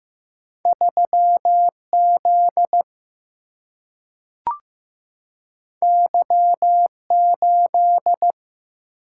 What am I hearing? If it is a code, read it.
Morse code, heard as a keyed tone, about 15 wpm, 700 Hz. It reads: 3 Z Y 8
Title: Single-Letter Single-Number 15wpm